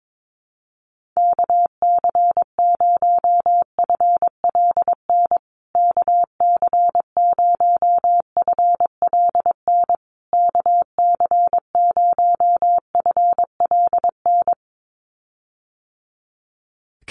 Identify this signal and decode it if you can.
Morse code, heard as a keyed tone, about 22 wpm, 700 Hz. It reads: KC0FLN KC0FLN KC0FLN
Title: USA Call Signs Repeated 3x 22wpm